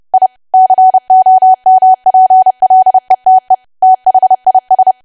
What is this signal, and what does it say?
Morse code, heard as a keyed tone, about 30 wpm, 750 Hz. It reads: I COMPLETE THIS
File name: cwa / W1AW-Level3 / 30WPM